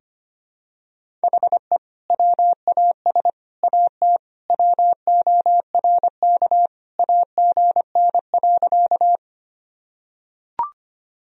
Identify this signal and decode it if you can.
Morse code, heard as a keyed tone, about 25 wpm, 700 Hz. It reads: HE WAS AT WORK AGN.
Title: Sentences from Top 200 Words and Abbreviations 25wpm